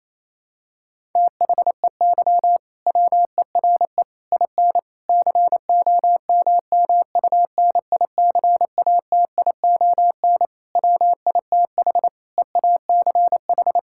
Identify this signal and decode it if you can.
Morse code, heard as a keyed tone, about 28 wpm, 700 Hz. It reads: THEY WERE IN COMMUNICATION WITH EACH
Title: Sentences from Top 1000 Words - Review Part 6 - 28wpm